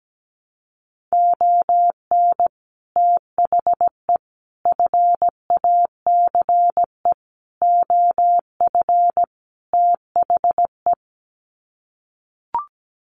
Morse code, heard as a keyed tone, about 17 wpm, 700 Hz. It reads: ON THE FACE OF THE
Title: Sets of 5 Words Encore2 17wpm